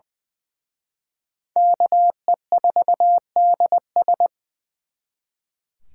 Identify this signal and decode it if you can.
Morse code, heard as a keyed tone, about 23 wpm, 700 Hz. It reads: KE4DS
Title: Speed Racing - USA 2x2 Call Signs 20wpm